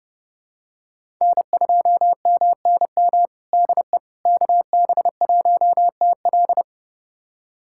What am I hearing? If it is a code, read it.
Morse code, heard as a keyed tone, about 30 wpm, 700 Hz. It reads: N2MNM DE KB1TL
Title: USA Callsign DE Callsign - 5 Characters - 30wpm